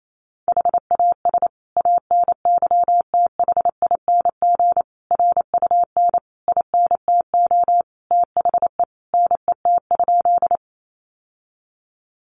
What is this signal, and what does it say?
Morse code, heard as a keyed tone, about 28 wpm, 700 Hz. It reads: HAS ANYTHING RUN INTO THE NET?
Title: Sentences from Top 1400 Words 28wpm